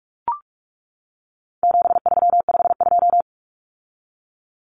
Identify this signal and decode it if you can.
Morse code, heard as a keyed tone, about 44 wpm, 700 Hz. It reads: 7352
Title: Numbers - 4-Digit 45wpm